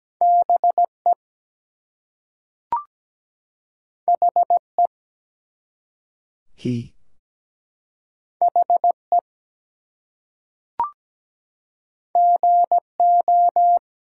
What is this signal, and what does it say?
Morse code, heard as a keyed tone, about 17 wpm, 700 Hz. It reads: BE HE HE GO
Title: Two Letter Words - 17wpm